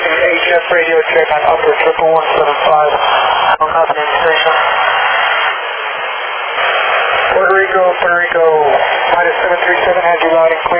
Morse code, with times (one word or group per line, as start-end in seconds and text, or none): none